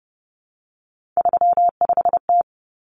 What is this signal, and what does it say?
Morse code, heard as a keyed tone, about 30 wpm, 700 Hz. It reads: 35T
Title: Common Cut Numbers - 3-Digit 30wpm